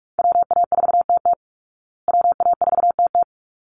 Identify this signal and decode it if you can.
Morse code, heard as a keyed tone, about 44 wpm, 700 Hz. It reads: WA4TT WA4TT
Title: Call Signs Repeated 3x - 45wpm